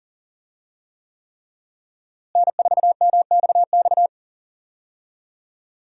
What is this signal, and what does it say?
Morse code, heard as a keyed tone, about 40 wpm, 700 Hz. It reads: N4MXX
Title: USA Call Signs - 5 Characters - Repeated 3x - 40wpm